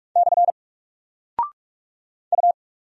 Morse code, heard as a keyed tone, about 44 wpm, 700 Hz. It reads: / U
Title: Single Character (Letter:Number:Punctuation:Prosign) 45wpm